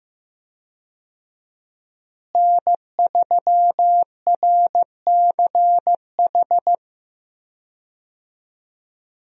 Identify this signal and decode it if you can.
Morse code, heard as a keyed tone, about 15 wpm, 700 Hz. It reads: N3RCH
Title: USA Call Signs - 5 Characters - Repeated 3x - 15wpm